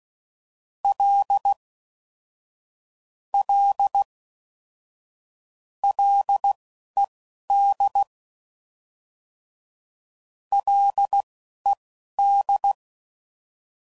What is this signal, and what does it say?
Morse code, heard as a keyed tone, about 16 wpm, spaced as for 11 wpm, 800 Hz. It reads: L L LED LED